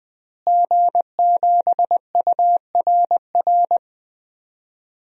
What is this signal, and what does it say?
Morse code, heard as a keyed tone, about 20 wpm, 700 Hz. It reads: G7URR
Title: DXCC-100 Call Signs - 5 Characters - Repeated 3x - 20wpm